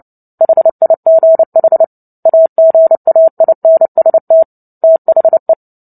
Morse code, heard as a keyed tone, about 30 wpm, 650 Hz. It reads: HIGH AGAINST THE
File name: Book Two / 03-Chapter 3 - 29 WPM